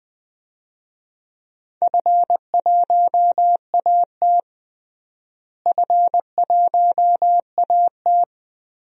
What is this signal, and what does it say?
Morse code, heard as a keyed tone, about 20 wpm, 700 Hz. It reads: F1AT F1AT
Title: DXCC-100 Call Signs - 4 Characters - Repeated 3x - 20wpm